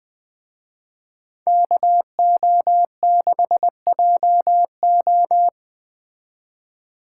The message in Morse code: KO6JO

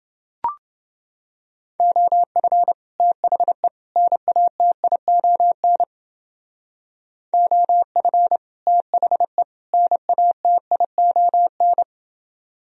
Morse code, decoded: OF THE NATION OF THE NATION

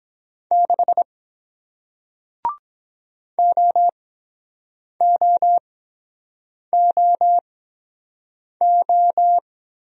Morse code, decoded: 6 OOOO

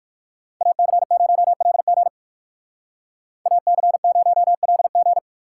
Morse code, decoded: AC0LG AC0LG

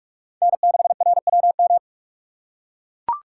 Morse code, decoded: N6RWM

